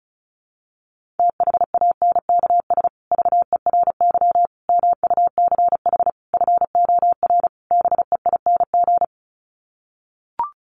Morse code, heard as a keyed tone, about 34 wpm, 700 Hz. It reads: THANKS VERY MUCH FOR BEING
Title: Sets of 5 Words Encore 35wpm